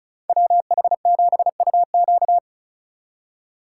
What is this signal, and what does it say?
Morse code, heard as a keyed tone, about 36 wpm, 700 Hz. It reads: WH7UQ
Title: USA Call Signs - 5 Characters - 3x Character Spacing - 35wpm